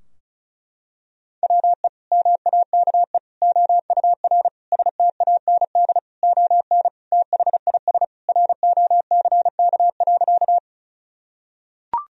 Morse code, heard as a keyed tone, about 34 wpm, 700 Hz. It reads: WE MAKE OUR STAND ON THIS ROCK.
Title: Sentences from Top 1000 Words - Review Part 2 - 35wpm